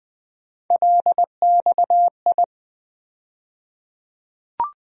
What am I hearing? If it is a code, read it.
Morse code, heard as a keyed tone, about 20 wpm, 700 Hz. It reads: LXI
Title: USA Call Sign Postfixes - 3 Letters 20wpm